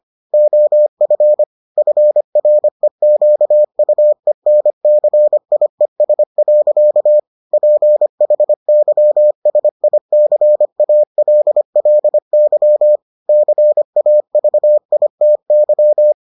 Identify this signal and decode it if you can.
Morse code, heard as a keyed tone, about 25 wpm, 600 Hz. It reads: OF FREQUENCIES. PHYSICALLY CAVITY